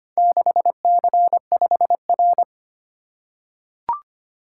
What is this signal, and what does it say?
Morse code, heard as a keyed tone, about 25 wpm, 700 Hz. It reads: 6C5R
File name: practice / 25wpm / Four Letter-Number Combinations - 25wpm